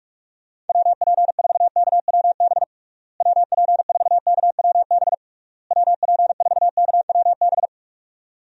Morse code, heard as a keyed tone, about 41 wpm, 700 Hz. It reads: WP4KWB WP4KWB WP4KWB
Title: USA Call Signs - 6 Characters - Repeated 3x - 45wpm